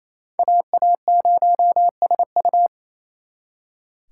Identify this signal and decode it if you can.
Morse code, heard as a keyed tone, about 35 wpm, 700 Hz. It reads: AA0SU
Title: Speed Racing - USA 2x2 Call Signs 28wpm